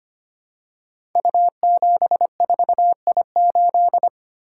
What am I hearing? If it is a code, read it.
Morse code, heard as a keyed tone, about 25 wpm, 700 Hz. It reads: U74I8
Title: Code Groups - 5 Letter:Numbers - 25wpm